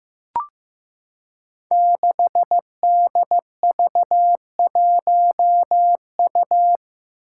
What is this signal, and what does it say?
Morse code, heard as a keyed tone, about 15 wpm, 700 Hz. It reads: 6DV1U
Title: Five Challenging Letter-Number Combinations 15wpm